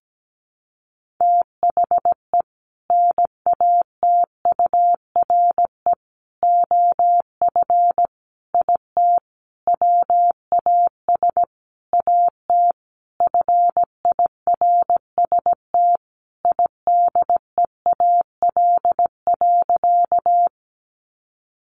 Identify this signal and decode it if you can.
Morse code, heard as a keyed tone, about 17 wpm, 700 Hz. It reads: THE NATURE OF IT WAS AT FIRST IDEAL.
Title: Sentences from Top 1400 Words 17wpm